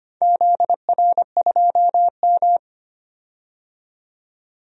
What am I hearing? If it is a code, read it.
Morse code, heard as a keyed tone, about 25 wpm, 700 Hz. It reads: ZR2M
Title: DXCC-100 Call Signs - 4 Characters - 25wpm